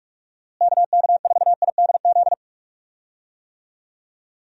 Morse code, heard as a keyed tone, about 42 wpm, 700 Hz. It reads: KK4IDZ
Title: USA Call Signs - 6 Characters - Repeated 2x - 45wpm